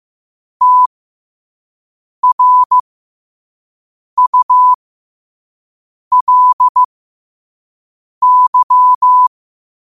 Morse code, heard as a keyed tone, about 15 wpm, spaced as for 5 wpm, 1000 Hz. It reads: TRULY
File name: qotdfiles / 5wpm